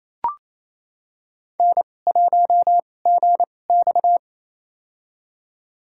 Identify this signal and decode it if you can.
Morse code, heard as a keyed tone, about 28 wpm, spaced as for 20 wpm, 700 Hz. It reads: N1GX
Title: USA Call Signs - 4 Characters - 2x Character Spacing - 28wpm